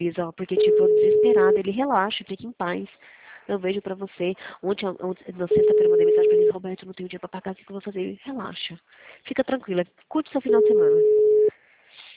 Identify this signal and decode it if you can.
Ringback tone (Europe, 425 Hz), with speech.
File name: out- -2018- - - .305